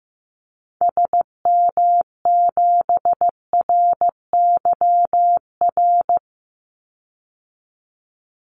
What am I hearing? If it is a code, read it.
Morse code, heard as a keyed tone, about 15 wpm, 700 Hz. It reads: SM7RYR
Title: Call Signs Repeated 2x - 15wpm